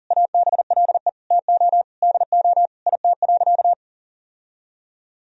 Morse code, heard as a keyed tone, about 40 wpm, 700 Hz. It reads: ABLE TO DO IT.
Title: Sentences from Top 1000 Words - Review Part 2 - 40wpm